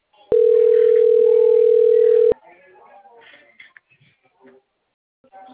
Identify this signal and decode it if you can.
Ringback tone (North America), 440 + 480 Hz.